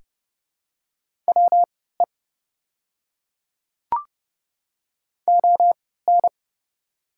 Morse code, heard as a keyed tone, about 30 wpm, spaced as for 17 wpm, 700 Hz. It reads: WE ON
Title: 3x Character Spacing - Top 100 Words - 30wpm